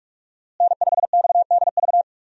Morse code, heard as a keyed tone, about 42 wpm, 700 Hz. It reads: N5XDV
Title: USA Call Signs - 5 Characters - Repeated 2x - 45wpm